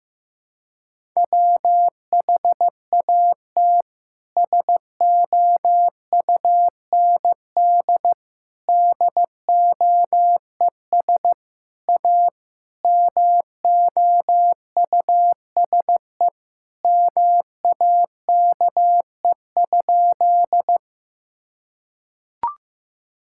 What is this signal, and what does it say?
Morse code, heard as a keyed tone, about 15 wpm, 700 Hz. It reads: WHAT SOUND DOES A MOUSE MAKE?
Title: Sentences from Top 1000 Words - Review Part 5 - 15wpm